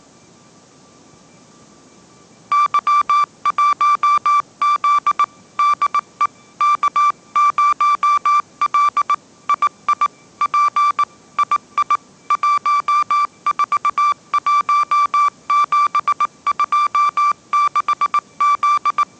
Morse code, heard as a keyed tone, about 18 wpm, 1170 Hz. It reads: Y1Z DE K0L II P II 141726Z